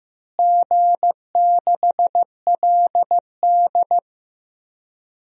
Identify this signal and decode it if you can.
Morse code, heard as a keyed tone, about 14 wpm, 700 Hz. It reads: G6LD
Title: DXCC-100 Call Signs - 4 Characters - Repeated 3x - 15wpm